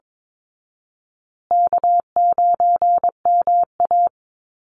Morse code, heard as a keyed tone, about 22 wpm, 700 Hz. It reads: K9MA
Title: Easy Call Signs Repeated 3x - 22wpm